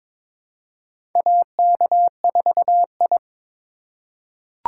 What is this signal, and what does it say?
Morse code, heard as a keyed tone, about 22 wpm, 700 Hz. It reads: AK4I
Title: USA Call Signs - 4 Characters - 3x Character Spacing - 22wpm